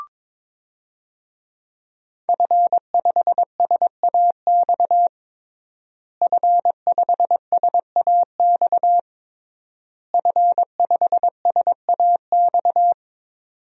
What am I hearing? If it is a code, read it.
Morse code, heard as a keyed tone, about 22 wpm, 700 Hz. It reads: F5SAX F5SAX F5SAX
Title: DXCC-100 Call Signs - 5 Characters - Repeated 3x - 22wpm